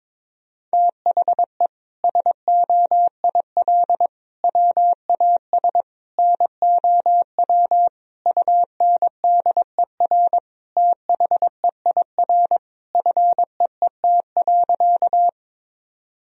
Morse code, heard as a keyed tone, about 22 wpm, 700 Hz. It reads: THE SOIL WAS NOW UNDER THEIR FEET.
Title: Sentences from Top 1000 Words - Review Part 4 - 22wpm